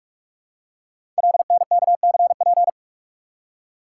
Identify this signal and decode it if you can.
Morse code, heard as a keyed tone, about 44 wpm, 700 Hz. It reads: LNKCP
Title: Code Groups - 5 Letter:Numbers - 45wpm